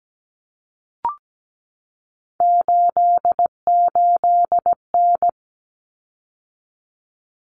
Morse code, heard as a keyed tone, about 17 wpm, 700 Hz. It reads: 88N
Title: Common Cut Numbers - 3-Digit 17wpm